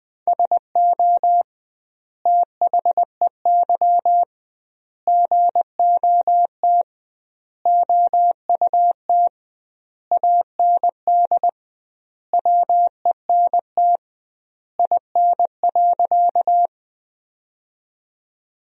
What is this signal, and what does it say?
Morse code, heard as a keyed tone, about 21 wpm, 700 Hz. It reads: SO THEY GOT OUT AND WENT IN.
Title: 2x Word-Spacing - Sentences from Top 300 Words - 20wpm